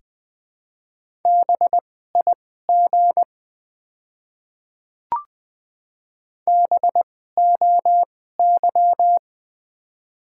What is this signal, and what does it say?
Morse code, heard as a keyed tone, about 20 wpm, spaced as for 15 wpm, 700 Hz. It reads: BIG BOY
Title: 2x Character Spacing - Top 200 Words - 20wpm